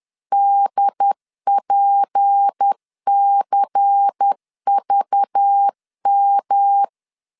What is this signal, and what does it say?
Morse code, heard as a keyed tone, about 10 wpm, 800 Hz. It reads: DPCVM
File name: LEZIONI / 8 YJ45 / 8 Y 10wpm 2615